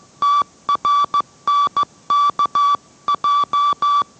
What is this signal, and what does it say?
Morse code, heard as a keyed tone, about 15 wpm, 1160 Hz. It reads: TRNKJ